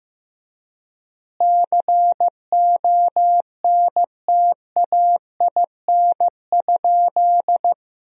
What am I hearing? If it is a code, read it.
Morse code, heard as a keyed tone, about 16 wpm, 700 Hz. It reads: CONTAIN?
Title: 3x Word-Spacing - Sentences from Top 500 Words - 15wpm